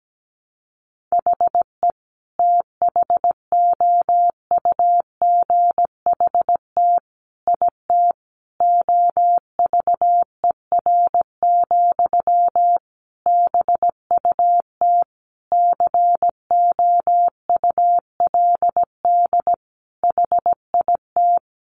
Morse code, heard as a keyed tone, about 17 wpm, 700 Hz. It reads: HE THOUGHT IT OVER, BUT COULD HIT
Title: Sentences from Top 1200 Words 17wpm